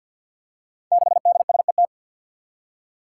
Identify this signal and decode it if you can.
Morse code, heard as a keyed tone, about 48 wpm, 700 Hz. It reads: 6DSET
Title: Five Challenging Letter-Number Combinations 50wpm